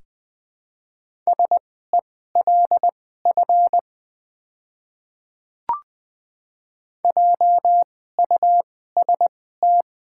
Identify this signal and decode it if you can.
Morse code, heard as a keyed tone, about 20 wpm, spaced as for 15 wpm, 690 Hz. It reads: SELF JUST
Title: 2x Character Spacing - Top 200 Words - 20wpm